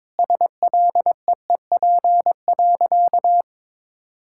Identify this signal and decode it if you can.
Morse code, heard as a keyed tone, about 22 wpm, 700 Hz. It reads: SLEEP.
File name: Sentences with 7-Word Phrases - 22wpm